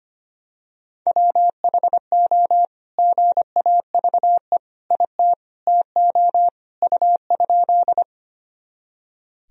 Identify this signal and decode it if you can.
Morse code, heard as a keyed tone, about 25 wpm, 700 Hz. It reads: WHO GAVE IT TO U?
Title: Sentences from Top 500 Words and Abbreviations 25wpm